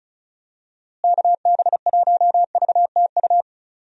Morse code, heard as a keyed tone, about 36 wpm, 700 Hz. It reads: KB1VTU
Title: USA Call Signs - 6 Characters - Repeated 3x - 35wpm